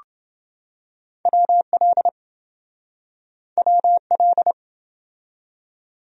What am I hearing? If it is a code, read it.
Morse code, heard as a keyed tone, about 26 wpm, 700 Hz. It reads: WL WL